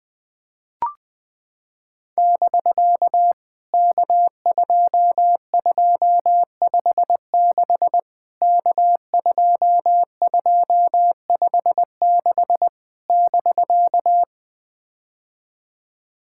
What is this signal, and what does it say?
Morse code, heard as a keyed tone, about 21 wpm, 700 Hz. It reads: <BK> K2256 K2256 <BK>
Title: POTA QSO Practice - 20wpm